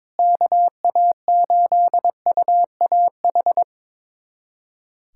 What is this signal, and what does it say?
Morse code, heard as a keyed tone, about 22 wpm, 700 Hz. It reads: KA8UAH
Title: USA Callsign DE Callsign 22wpm